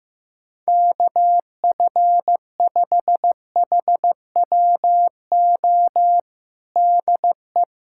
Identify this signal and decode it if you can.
Morse code, heard as a keyed tone, about 16 wpm, 700 Hz. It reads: KF5HWO DE
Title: USA Callsign DE Callsign - 6 Characters - 15wpm